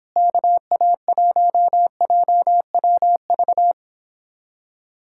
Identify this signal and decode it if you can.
Morse code, heard as a keyed tone, about 29 wpm, 700 Hz. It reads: KA1JWV